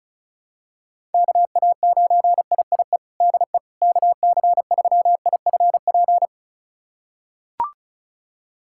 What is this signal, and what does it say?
Morse code, heard as a keyed tone, about 36 wpm, 700 Hz. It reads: KA9IIE DE KC3IFP
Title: USA Callsign DE Callsign 35wpm